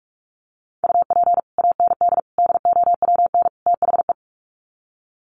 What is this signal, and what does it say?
Morse code, heard as a keyed tone, about 44 wpm, 700 Hz. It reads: UP AND DOWN THE
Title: Sets of 4 Words 45wpm